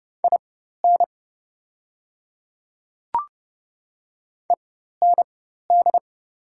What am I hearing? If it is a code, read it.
Morse code, heard as a keyed tone, about 30 wpm, spaced as for 14 wpm, 700 Hz. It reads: IN END
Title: 4x Character Spacing - Top 500 Words - 30wpm